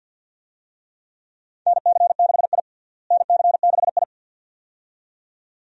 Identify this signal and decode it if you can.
Morse code, heard as a keyed tone, about 47 wpm, 700 Hz. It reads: NC6I NC6I
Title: USA Call Signs - 4 Characters - Repeated 2x - 50wpm